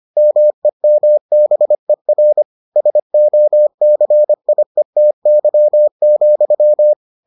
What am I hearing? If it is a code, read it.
Morse code, heard as a keyed tone, about 25 wpm, 600 Hz. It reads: MEMBER SOCIETY,